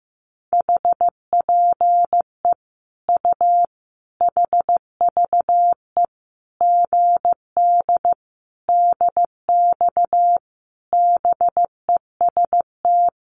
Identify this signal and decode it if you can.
Morse code, heard as a keyed tone, about 15 wpm, 700 Hz. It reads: HPE U HVE GD DX BEST
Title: Common QSO Phrases 15wpm